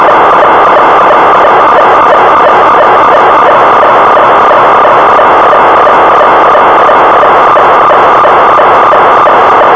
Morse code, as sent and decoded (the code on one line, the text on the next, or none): none